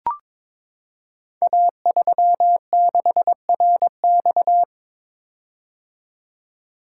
Morse code, heard as a keyed tone, about 23 wpm, 700 Hz. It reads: A36RX
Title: Code Groups - 5 Letter:Numbers - 22wpm